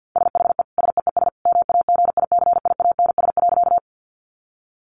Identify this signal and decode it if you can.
Morse code, heard as a keyed tone, about 50 wpm, 700 Hz. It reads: SHE SEES MAGICIANS.
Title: Easy Phrases 50wpm